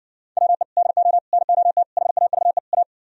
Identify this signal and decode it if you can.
Morse code, heard as a keyed tone, about 60 wpm, 700 Hz. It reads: WE DO NOT HAVE A